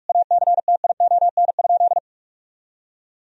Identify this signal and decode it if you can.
Morse code, heard as a keyed tone, about 46 wpm, 700 Hz. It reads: ACTION?
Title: Sentences from Top 1000 Words - Review Part 4 - 45wpm